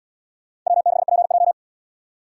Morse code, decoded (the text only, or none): W6PJ